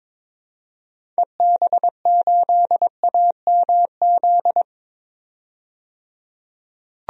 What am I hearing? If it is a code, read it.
Morse code, heard as a keyed tone, about 22 wpm, 700 Hz. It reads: EB8AMZ
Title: DXCC-100 Call Signs - 6 Characters - 22wpm